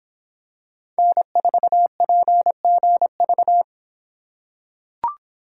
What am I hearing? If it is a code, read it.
Morse code, heard as a keyed tone, about 24 wpm, 700 Hz. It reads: N4PGV